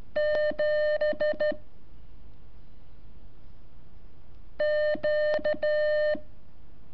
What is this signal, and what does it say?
Morse code, heard as a keyed tone, about 12 wpm, 600 Hz. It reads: 7 Q
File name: curso / ejercicio 22